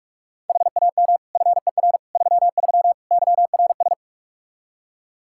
Morse code, heard as a keyed tone, about 41 wpm, 700 Hz. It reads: HAM FER 33 YRS
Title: Practice QSO - Exchange 3 - More Info - 45wpm